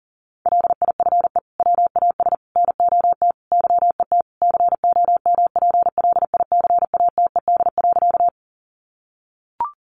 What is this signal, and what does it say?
Morse code, heard as a keyed tone, about 40 wpm, 700 Hz. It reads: LIFE WAS NOT YET COMPLICATED.
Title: Sentences from Top 1200 Words 40wpm